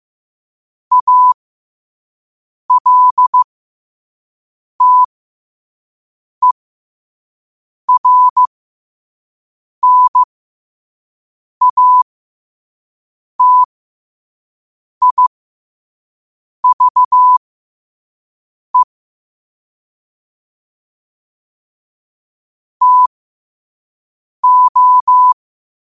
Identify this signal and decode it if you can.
Morse code, heard as a keyed tone, about 15 wpm, spaced as for 5 wpm, 1000 Hz. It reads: ALTERNATIVE TO
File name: qotdfiles / 5wpm